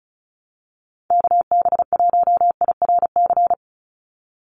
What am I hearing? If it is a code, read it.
Morse code, heard as a keyed tone, about 34 wpm, 700 Hz. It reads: KB1IRC
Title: USA Call Signs 35wpm